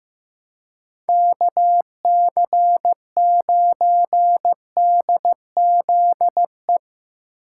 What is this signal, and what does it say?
Morse code, heard as a keyed tone, about 17 wpm, 700 Hz. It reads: KC9DZE
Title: Speed Racing - USA Call Signs 15wpm